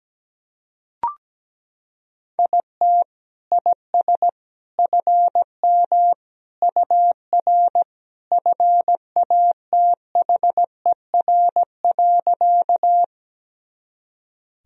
Morse code, heard as a keyed tone, about 17 wpm, 700 Hz. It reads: IT IS FM UR FATHER.
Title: Sentences from Top 300 Words and Abbreviations 17wpm